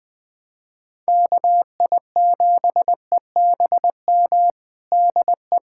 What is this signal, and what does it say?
Morse code, heard as a keyed tone, about 21 wpm, 700 Hz. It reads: KI7EBM DE
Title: USA Callsign DE Callsign - 6 Characters - 20wpm